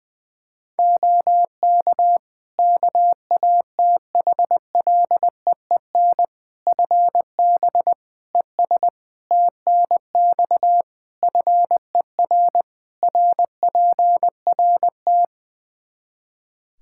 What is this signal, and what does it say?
Morse code, heard as a keyed tone, about 21 wpm, 700 Hz. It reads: OK KATHLEEN FB ES TNX FER RPRT
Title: Practice QSO - Exchange 2 - Info - 20wpm